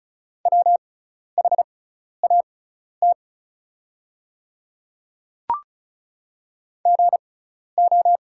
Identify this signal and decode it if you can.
Morse code, heard as a keyed tone, about 36 wpm, spaced as for 12 wpm, 700 Hz. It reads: WHAT GO
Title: 6x Character Spacing - Top 100 Words - 35wpm